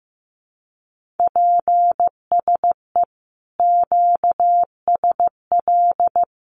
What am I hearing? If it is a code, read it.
Morse code, heard as a keyed tone, about 15 wpm, 700 Hz. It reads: PSE QSL
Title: Common QSO Phrases Encore 15wpm